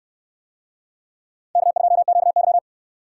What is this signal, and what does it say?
Morse code, heard as a keyed tone, about 65 wpm, 700 Hz. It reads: K2QY